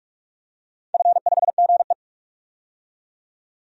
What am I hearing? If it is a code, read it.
Morse code, heard as a keyed tone, about 46 wpm, 700 Hz. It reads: F5GE
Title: DXCC-100 Call Signs - 4 Characters - 45wpm